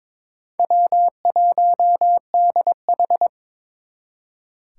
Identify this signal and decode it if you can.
Morse code, heard as a keyed tone, about 26 wpm, 700 Hz. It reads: W1DH
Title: Speed Racing - USA 1x2 Call Signs 22wpm